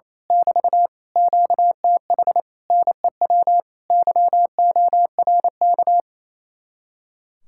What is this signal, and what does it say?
Morse code, heard as a keyed tone, about 27 wpm, 700 Hz. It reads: <BT> QTH NEW YORK
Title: Speed-Racing Common QSO Phrases Encore - 28wpm